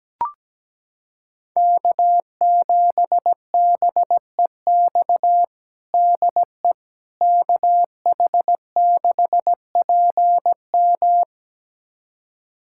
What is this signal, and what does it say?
Morse code, heard as a keyed tone, about 17 wpm, 700 Hz. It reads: K7BEX DE KH6PM
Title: USA Callsign DE Callsign - 5 Characters - 17wpm